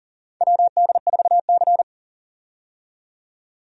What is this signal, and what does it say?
Morse code, heard as a keyed tone, about 40 wpm, 700 Hz. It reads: WD4C
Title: USA Call Signs - 4 Characters - 40wpm